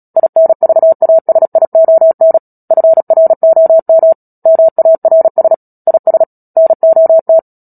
Morse code, heard as a keyed tone, about 37 wpm, 650 Hz. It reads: INVASION FROM MARS IS NOT
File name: Book Two / 10-Chapter 10 - Epilogue - 36 WPM